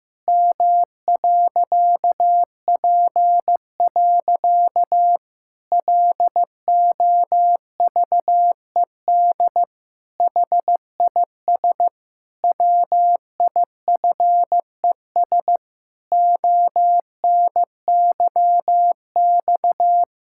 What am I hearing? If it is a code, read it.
Morse code, heard as a keyed tone, about 14 wpm, 700 Hz. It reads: M.P. LOVED HIS WIFES ONYX
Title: Pangram Challenge v2 - 15wpm